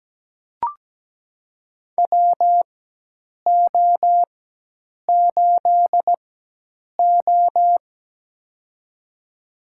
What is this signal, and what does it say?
Morse code, heard as a keyed tone, about 17 wpm, spaced as for 8 wpm, 700 Hz. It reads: WO8O
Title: USA Call Signs - 4 Characters - 4x Character Spacing - 17wpm